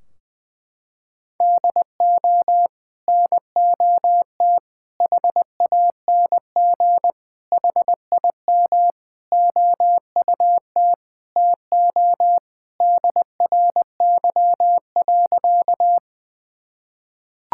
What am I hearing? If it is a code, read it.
Morse code, heard as a keyed tone, about 20 wpm, 700 Hz. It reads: DO NOT HANG HIM OUT TO DRY.
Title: Sentences from Top 1600 Words 20wpm